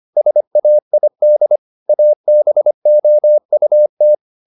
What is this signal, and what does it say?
Morse code, heard as a keyed tone, about 25 wpm, 600 Hz. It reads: SAID ABOUT